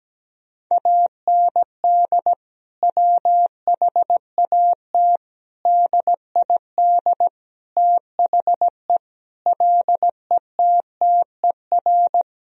Morse code, heard as a keyed tone, about 17 wpm, 700 Hz. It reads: AND WHAT DID THE LETTER
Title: Sentences from Top 1000 Words - Review Part 2 - 17wpm